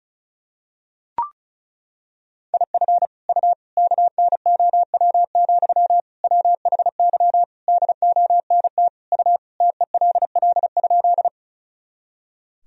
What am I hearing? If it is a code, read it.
Morse code, heard as a keyed tone, about 36 wpm, 700 Hz. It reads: IF U KNOW, WHY DONT U TELL?
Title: Sentences from Top 300 Words and Abbreviations 35wpm